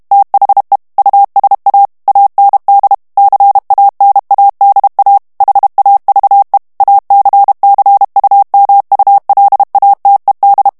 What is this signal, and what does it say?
Morse code, heard as a keyed tone, about 32 wpm, 800 Hz. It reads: THE USA AND CANADA HAVE ACCUMULATED
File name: ANGLAIS texte clair 2 vitesse 30 mots minute